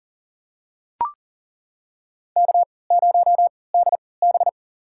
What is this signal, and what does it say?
Morse code, heard as a keyed tone, about 40 wpm, spaced as for 23 wpm, 700 Hz. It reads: K0DB
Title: USA Call Signs - 4 Characters - 3x Character Spacing - 40wpm